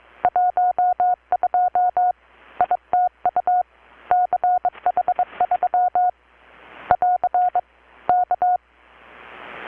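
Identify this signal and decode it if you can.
Morse code, heard as a keyed tone, about 22 wpm, 720 Hz. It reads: 12 ITU CH3 <AR> K
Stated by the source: Morse from KPH's Night of Nights XX maritime radiotelegraphy event, received in Chiapas, Mexico about 3,800 km away on a Kenwood R-600 with a grounded 10 m long-wire antenna.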